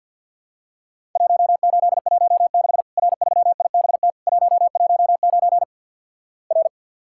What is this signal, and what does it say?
Morse code, heard as a keyed tone, about 50 wpm, 700 Hz. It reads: 1816 R3IBT 119 R
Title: Worked All Europe DX Contest - QTC Exchange with Repeats - 50wpm